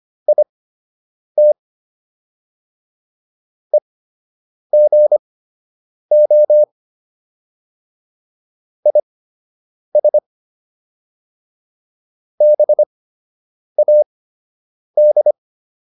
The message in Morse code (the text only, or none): IT EGO IS BAD